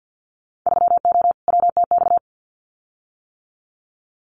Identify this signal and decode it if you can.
Morse code, heard as a keyed tone, about 50 wpm, 700 Hz. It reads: 3O WTX
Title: Winter Field Day Exchanges Encore 50wpm